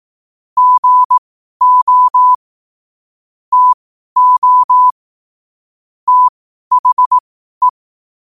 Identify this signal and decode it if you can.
Morse code, heard as a keyed tone, about 18 wpm, spaced as for 12 wpm, 1000 Hz. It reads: GO TO THE
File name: qotdfiles / 12wpm